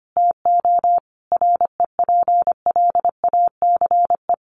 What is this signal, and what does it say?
Morse code, heard as a keyed tone, about 25 wpm, 700 Hz. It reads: TO REPLACE
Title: Sentences from Top 1400 Words 25wpm